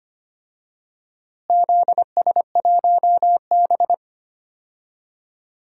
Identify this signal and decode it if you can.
Morse code, heard as a keyed tone, about 25 wpm, 700 Hz. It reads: ZS1B
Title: DXCC-100 Call Signs - 4 Characters - Repeated 2x - 25wpm